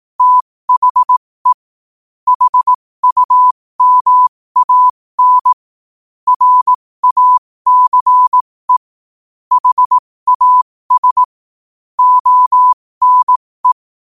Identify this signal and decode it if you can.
Morse code, heard as a keyed tone, about 18 wpm, spaced as for 15 wpm, 1000 Hz. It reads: THE HUMAN RACE HAS ONE
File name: qotdfiles / 15wpm